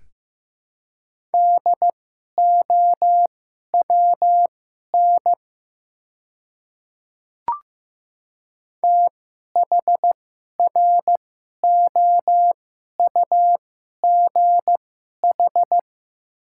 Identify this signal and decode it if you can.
Morse code, heard as a keyed tone, about 15 wpm, spaced as for 11 wpm, 700 Hz. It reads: DOWN THROUGH
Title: 2x Character Spacing - Top 300 Words - 15wpm